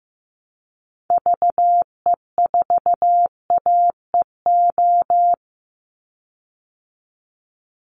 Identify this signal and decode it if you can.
Morse code, heard as a keyed tone, about 15 wpm, 700 Hz. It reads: VE4AEO
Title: Call Signs Repeated 2x - 15wpm